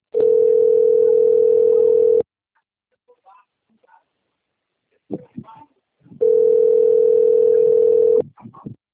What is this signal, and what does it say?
Ringback tone (North America), 440 + 480 Hz.
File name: rg-1600-unknown- - - .11542